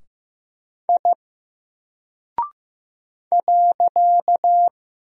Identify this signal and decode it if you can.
Morse code, heard as a keyed tone, about 15 wpm, 700 Hz. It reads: I .
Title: Single Character (Letter:Number:Punctuation:Prosign) Alt 15wpm